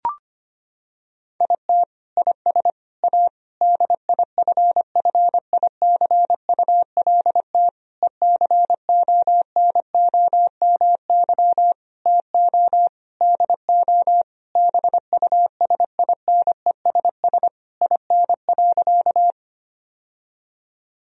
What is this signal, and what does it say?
Morse code, heard as a keyed tone, about 25 wpm, 700 Hz. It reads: IT IS A DIFFICULT ECONOMY TO DO BUSINESS IN.
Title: Sentences from Top 1000 Words - Review Part 3 - 25wpm